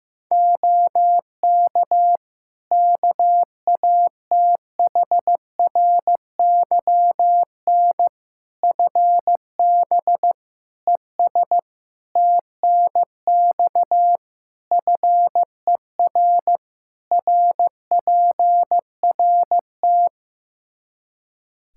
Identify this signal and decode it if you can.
Morse code, heard as a keyed tone, about 15 wpm, 700 Hz. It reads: OK KATHRYN FB ES TNX FER RPRT
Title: K1USN SST QSO Practice 15wpm